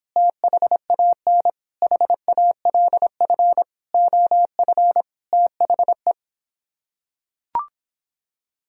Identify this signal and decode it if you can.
Morse code, heard as a keyed tone, about 26 wpm, 700 Hz. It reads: THAN HALF OF THE